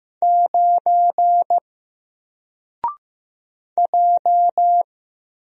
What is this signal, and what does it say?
Morse code, heard as a keyed tone, about 15 wpm, 700 Hz. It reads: 9 J
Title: Single Character (Letter:Number:Punctuation:Prosign) Alt 15wpm